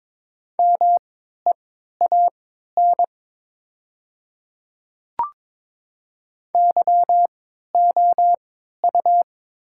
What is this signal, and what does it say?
Morse code, heard as a keyed tone, about 22 wpm, spaced as for 13 wpm, 700 Hz. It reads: MEAN YOU
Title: 3x Character Spacing - Top 200 Words - 22wpm